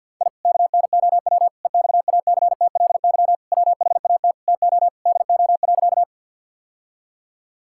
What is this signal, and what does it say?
Morse code, heard as a keyed tone, about 50 wpm, 700 Hz. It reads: I KNOW EXACTLY WHAT TO DO.
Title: Sentences from Top 1000 Words - Review Part 6 - 50wpm